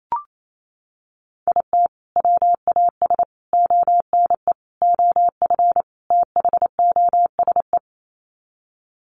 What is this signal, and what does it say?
Morse code, heard as a keyed tone, about 28 wpm, 700 Hz. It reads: IT WAS ONE OF THOSE
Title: Sets of 5 Words Encore 28wpm